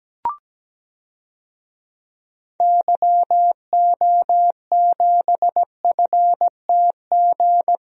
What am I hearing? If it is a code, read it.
Morse code, heard as a keyed tone, about 17 wpm, 700 Hz. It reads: YO7FTG
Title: DXCC-100 Call Signs - 6 Characters - Repeated 3x - 17wpm